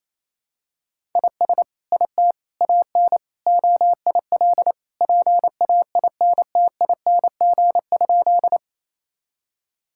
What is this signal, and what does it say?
Morse code, heard as a keyed tone, about 28 wpm, 700 Hz. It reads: IS IT AN OIL PAINTING?
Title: Sentences from Top 1000 Words - Review Part 6 - 28wpm